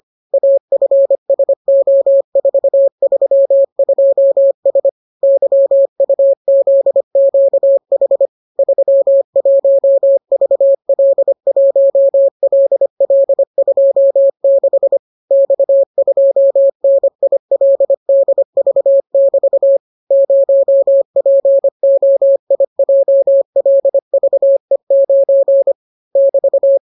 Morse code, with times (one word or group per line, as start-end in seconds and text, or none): 0.3-4.9 AFSO432S
5.2-8.3 YUZQH
8.6-15.0 31VL1LL26
15.3-19.8 X2NILDV<BT>
20.1-25.7 0POIJLVE9
26.2-26.8 <BT>